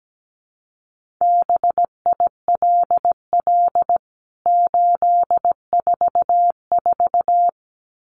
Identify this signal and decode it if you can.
Morse code, heard as a keyed tone, about 17 wpm, 700 Hz. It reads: BILL 844
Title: CWT Exchange 17wpm